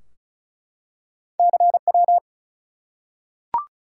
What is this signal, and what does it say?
Morse code, heard as a keyed tone, about 36 wpm, 700 Hz. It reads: CW
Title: Top 180 QSO Elements 35wpm